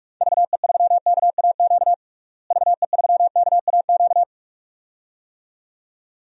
Morse code, heard as a keyed tone, about 41 wpm, 700 Hz. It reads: VE3KAQ VE3KAQ
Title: DXCC-100 Call Signs - 6 Characters - Repeated 2x - 45wpm